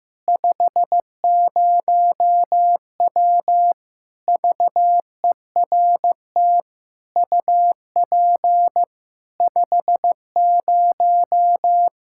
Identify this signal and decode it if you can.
Morse code, heard as a keyed tone, about 15 wpm, 700 Hz. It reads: 50W VERT UP 50
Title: Practice QSO - Exchange 2 - Info - 15wpm